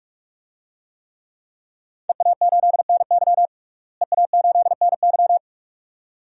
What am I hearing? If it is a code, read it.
Morse code, heard as a keyed tone, about 46 wpm, 700 Hz. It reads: EA8NY EA8NY
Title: DXCC-100 Call Signs - 5 Characters - Repeated 2x - 45wpm